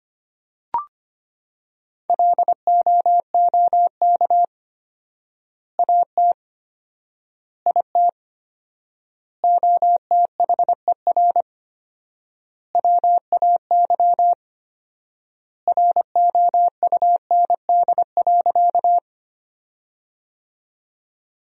Morse code, heard as a keyed tone, about 25 wpm, 700 Hz. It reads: LOOK AT IT OTHER WAY ROUND.
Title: 4x Word-Spacing - Sentences from Top 200 Words - 25wpm